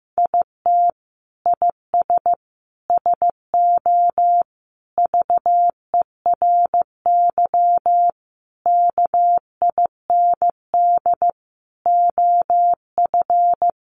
Morse code, heard as a keyed tone, about 15 wpm, 700 Hz. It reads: IT IS SO VERY KIND OF
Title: Sentences from Top 300 Words 15wpm